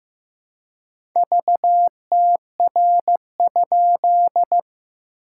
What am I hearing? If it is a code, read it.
Morse code, heard as a keyed tone, about 15 wpm, 690 Hz. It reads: VTR?
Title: SOTA QSO Practice 15wpm